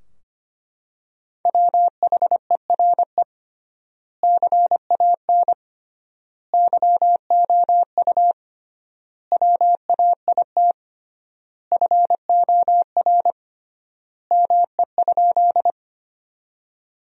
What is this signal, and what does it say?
Morse code, heard as a keyed tone, about 25 wpm, 700 Hz. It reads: WHERE CAN YOU WAIT FOR ME?
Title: 3x Word-Spacing - Sentences from Top 500 Words - 25wpm